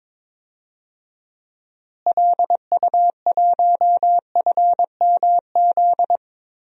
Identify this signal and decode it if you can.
Morse code, heard as a keyed tone, about 22 wpm, 700 Hz. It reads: LU1FMZ
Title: DXCC-100 Call Signs - 6 Characters - Repeated 3x - 22wpm